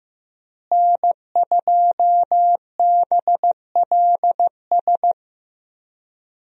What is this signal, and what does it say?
Morse code, heard as a keyed tone, about 15 wpm, 700 Hz. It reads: N2BLS
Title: USA Call Signs - 5 Characters - Repeated 3x - 15wpm